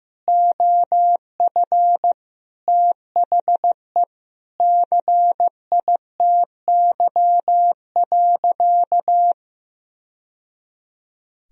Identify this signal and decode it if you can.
Morse code, heard as a keyed tone, about 15 wpm, 700 Hz. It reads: OF THE CITY.
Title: Sentences from Top 1000 Words - Review Part 1 - 15wpm